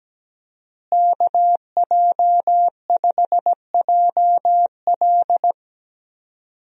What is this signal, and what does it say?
Morse code, heard as a keyed tone, about 17 wpm, 700 Hz. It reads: KJ5JL
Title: USA Call Signs - 5 Characters - Repeated 3x - 17wpm